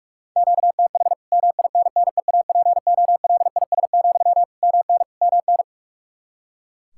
Morse code, heard as a keyed tone, about 47 wpm, 700 Hz. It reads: QTH MINNEAPOLIS, MN MN
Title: Practice QSO - Exchange 1 - Beginning - 45wpm